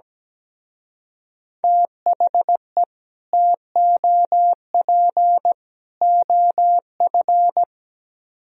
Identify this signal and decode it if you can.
Morse code, heard as a keyed tone, about 17 wpm, 700 Hz. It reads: THE TOP OF
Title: Speed-Racing - Sets of 3 Words 17wpm